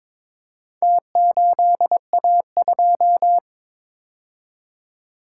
Morse code, heard as a keyed tone, about 21 wpm, 700 Hz. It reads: T8A2
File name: practice / 22wpm / Four Letter-Number Combinations - 22wpm